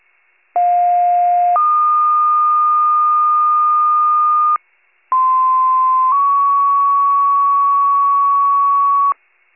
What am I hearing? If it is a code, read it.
Two-tone page, heard as a one-second tone, then a three-second tone, about 710 then 1220 Hz, about 1040 then 1120 Hz. It is heard from 0.5 to 9.1 s.
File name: pager / 46 RESCUE / wcalert